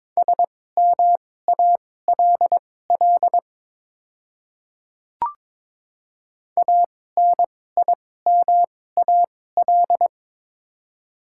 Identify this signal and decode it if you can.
Morse code, heard as a keyed tone, about 22 wpm, spaced as for 16 wpm, 700 Hz. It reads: SMALL ANIMAL
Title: 2x Character Spacing - Top 200 Words - 22wpm